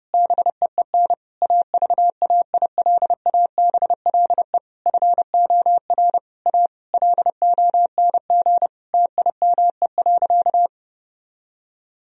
Morse code, heard as a keyed tone, about 32 wpm, 700 Hz. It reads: BEEN AVAILABLE FOR A LONG TIME.
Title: Sentences from Top 1000 Words - Review Part 4 - 30wpm